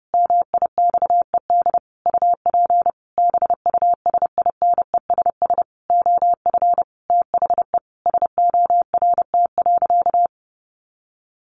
Morse code, heard as a keyed tone, about 28 wpm, 700 Hz. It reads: MIXED UP BUSINESS OF THE SORT.
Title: Sentences from Top 1500 Words 30wpm